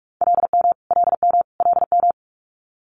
Morse code, heard as a keyed tone, about 46 wpm, 700 Hz. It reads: LM LM LM
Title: Multi-letter Phonemes Repeated 3x 45wpm